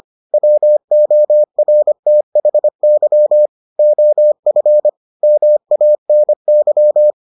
Morse code, heard as a keyed tone, about 25 wpm, 600 Hz. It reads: WORTHY OF MANY